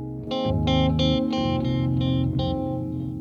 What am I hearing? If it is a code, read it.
There is no code.